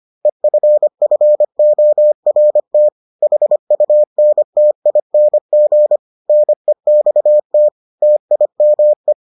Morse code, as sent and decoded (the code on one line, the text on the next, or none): . ..-. ..-. --- .-. - / .... ..- -. - .. -. --. / -. -..- - / - .. -- .
EFFORT HUNTING NEXT TIME